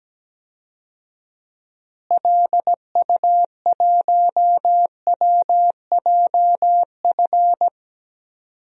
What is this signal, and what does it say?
Morse code, heard as a keyed tone, about 17 wpm, 700 Hz. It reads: LU1WJF
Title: DXCC-100 Call Signs - 6 Characters - Repeated 2x - 17wpm